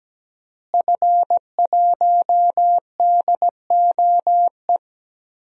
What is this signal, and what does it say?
Morse code, heard as a keyed tone, about 17 wpm, 700 Hz. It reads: F1DOE
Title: DXCC-100 Call Signs - 5 Characters - Repeated 2x - 17wpm